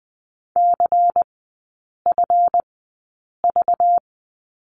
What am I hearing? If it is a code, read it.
Morse code, heard as a keyed tone, about 20 wpm, spaced as for 8 wpm, 700 Hz. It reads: CFV
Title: 191 Lesson 31e - Letter v - Three Characters 20wpm